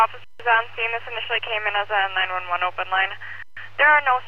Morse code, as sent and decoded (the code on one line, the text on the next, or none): none